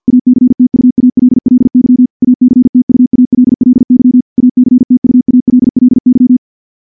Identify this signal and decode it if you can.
Morse code, heard as a keyed tone, about 50 wpm, 250 Hz. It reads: ACTUALLY ACTUALLY ACTUALLY
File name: morse / morse-training-50WPM